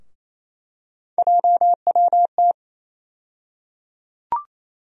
Morse code, heard as a keyed tone, about 28 wpm, 700 Hz. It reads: JWT
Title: USA Call Sign Postfixes - 3 Letters 28wpm